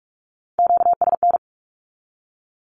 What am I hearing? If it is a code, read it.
Morse code, heard as a keyed tone, about 46 wpm, 700 Hz. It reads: QSN
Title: Q-Codes with Spoken Meaning - 45wpm